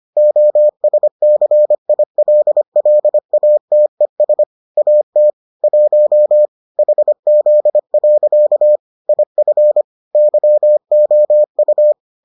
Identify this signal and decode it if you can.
Morse code, heard as a keyed tone, about 25 wpm, 600 Hz. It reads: OSCILLATES AT 1 HZ. IF YOU